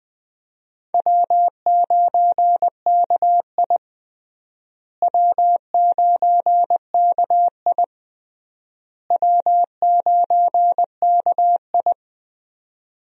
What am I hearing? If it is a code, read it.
Morse code, heard as a keyed tone, about 21 wpm, 700 Hz. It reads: W9KI W9KI W9KI
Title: USA Call Signs - 4 Characters - Repeated 3x - 20wpm